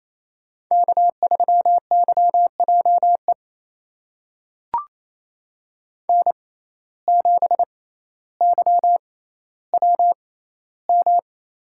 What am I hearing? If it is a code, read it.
Morse code, heard as a keyed tone, about 28 wpm, spaced as for 14 wpm, 700 Hz. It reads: K3YJE N7YWM
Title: USA Call Signs - 5 Characters - 6x Character Spacing - 28wpm